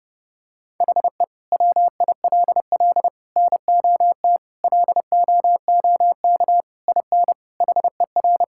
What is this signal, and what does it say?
Morse code, heard as a keyed tone, about 31 wpm, 700 Hz. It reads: HE WILL NOT LOOK IN HER
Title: Sentences from Top 1000 Words - Review Part 5 - 30wpm